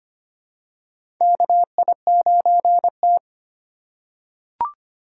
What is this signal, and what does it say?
Morse code, heard as a keyed tone, about 25 wpm, 700 Hz. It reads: KI9T
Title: USA Call Signs - 2x Character Spacing - 25wpm